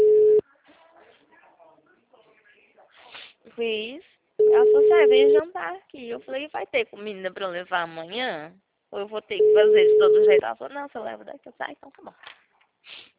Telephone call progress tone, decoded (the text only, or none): ringback tone